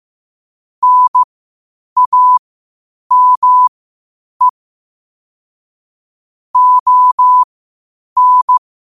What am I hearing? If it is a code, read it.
Morse code, heard as a keyed tone, about 15 wpm, spaced as for 8 wpm, 1000 Hz. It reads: NAME ON
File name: qotdfiles / 8wpm